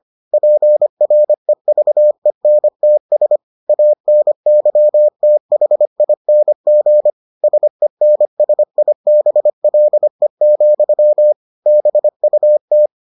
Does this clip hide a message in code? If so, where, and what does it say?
Yes: Morse code from 0.3 to 12.9 s, heard as a keyed tone, about 25 wpm, 600 Hz. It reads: PREVENTS ANYTHING SENSIBLE, BUT